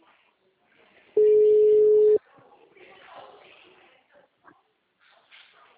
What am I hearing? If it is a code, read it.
Ringback tone (Europe, 425 Hz), with speech.